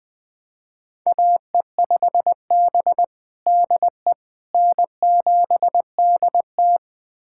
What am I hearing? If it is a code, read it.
Morse code, heard as a keyed tone, about 20 wpm, 700 Hz. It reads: AE5B DE N7DT
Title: USA Callsign DE Callsign - 4 Characters - 20wpm